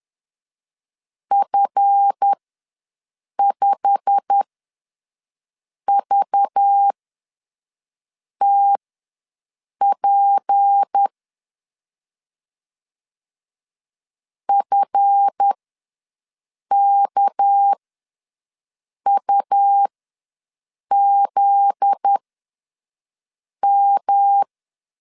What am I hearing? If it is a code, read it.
Morse code, heard as a keyed tone, about 10 wpm, spaced as for 5 wpm, 800 Hz. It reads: F5VTP FKUZM
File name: LEZIONI / 8 YJ45 / 8 YJ45 ws 10wpm 3843